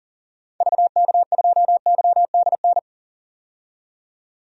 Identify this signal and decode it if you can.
Morse code, heard as a keyed tone, about 38 wpm, 700 Hz. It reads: VK2YDN
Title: DXCC-100 Call Signs - 6 Characters - 40wpm